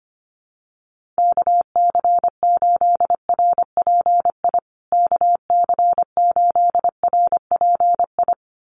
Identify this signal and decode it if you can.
Morse code, heard as a keyed tone, about 25 wpm, 700 Hz. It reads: KC8RPI KC8RPI
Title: USA Call Signs Repeated 2x 25wpm